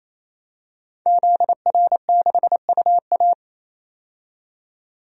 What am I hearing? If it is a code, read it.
Morse code, heard as a keyed tone, about 28 wpm, 700 Hz. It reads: ZR6UA
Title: DXCC-100 Call Signs - 5 Characters - 28wpm